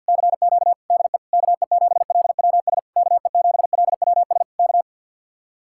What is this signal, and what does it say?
Morse code, heard as a keyed tone, about 64 wpm, spaced as for 57 wpm, 700 Hz. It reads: CQ DE KE7LWS KE7LWS K
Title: Speed-Racing Common QSO Phrases Encore - 50wpm